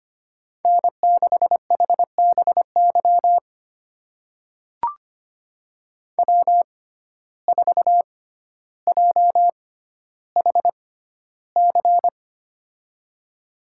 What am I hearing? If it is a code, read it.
Morse code, heard as a keyed tone, about 25 wpm, spaced as for 13 wpm, 700 Hz. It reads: N6HBY W4JHC
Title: USA Call Signs - 5 Characters - 6x Character Spacing - 25wpm